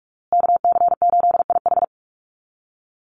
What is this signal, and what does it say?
Morse code, heard as a keyed tone, about 44 wpm, 690 Hz. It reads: KC8IH
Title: USA Call Signs Repeated 2x 45wpm